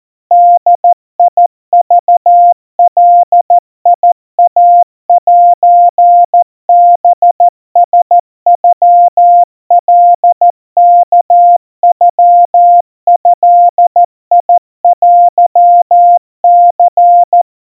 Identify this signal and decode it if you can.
Morse code (Japanese Wabun code), heard as a keyed tone, about 14 wpm, 700 Hz. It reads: ボクガイセハラノカワノドテニ
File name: kawa / Morse sample / kamyu 50 cpm 700 Hz Crispy